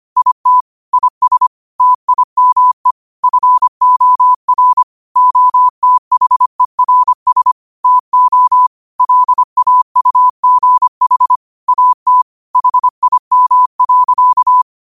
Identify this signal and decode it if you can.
Morse code, heard as a keyed tone, about 25 wpm, 1000 Hz. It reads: IT IS TIME FOR OTHERS TO LAUGH AT HIM.
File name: qotdfiles / 25wpm